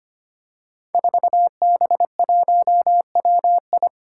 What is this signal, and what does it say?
Morse code, heard as a keyed tone, about 25 wpm, 700 Hz. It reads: 4B1WI
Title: Five Challenging Letter-Number Combinations 25wpm